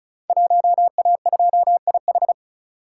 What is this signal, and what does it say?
Morse code, heard as a keyed tone, about 36 wpm, 690 Hz. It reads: 1A2IH
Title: Code Groups - 5 Letter:Numbers - 35wpm